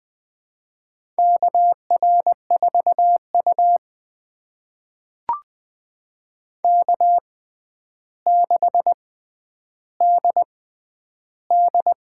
Morse code, heard as a keyed tone, about 21 wpm, spaced as for 10 wpm, 700 Hz. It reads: KR4U K6DD
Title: USA Call Signs - 4 Characters - 6x Character Spacing - 20wpm